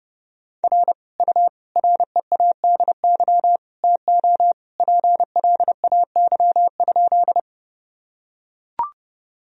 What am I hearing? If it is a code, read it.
Morse code, heard as a keyed tone, about 30 wpm, 700 Hz. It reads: R U READY TO PLAY?
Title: Sentences from Top 400 Words and Abbreviations 30wpm